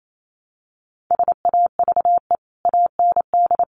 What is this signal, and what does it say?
Morse code, heard as a keyed tone, about 28 wpm, 700 Hz. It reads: SAVE AND
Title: Sentences from Top 1100 Words 28wpm